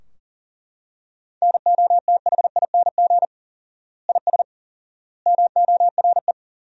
Morse code, heard as a keyed tone, about 40 wpm, 700 Hz. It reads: NOTHING IS MORE
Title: 4x Word-Spacing - Sentences from Top 500 Words - 40wpm